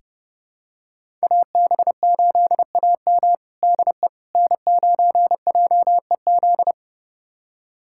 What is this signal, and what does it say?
Morse code, heard as a keyed tone, about 30 wpm, 700 Hz. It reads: AB8AM DE N9JEZ
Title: USA Callsign DE Callsign - 5 Characters - 30wpm